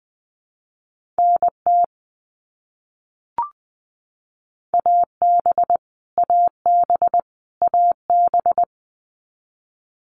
Morse code, heard as a keyed tone, about 20 wpm, 700 Hz. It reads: NT AB AB AB
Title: Canadian Provinces and Territories - Repeated 3x - 20wpm